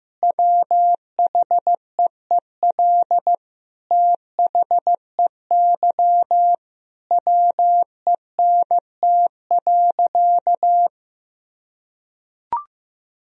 Morse code, heard as a keyed tone, about 16 wpm, 700 Hz. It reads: WHEEL THEY WENT.
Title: Sentences from Top 1000 Words - Review Part 3 - 15wpm